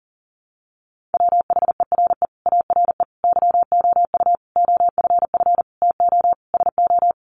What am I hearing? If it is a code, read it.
Morse code, heard as a keyed tone, about 37 wpm, 700 Hz. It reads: WHERE ARE YOU OFF TO SO
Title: Sentences from Top 400 Words 40wpm